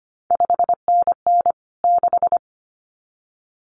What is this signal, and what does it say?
Morse code, heard as a keyed tone, about 25 wpm, 700 Hz. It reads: 5NN 6
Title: CQ WW DX Contest Exchange Encore - 25wpm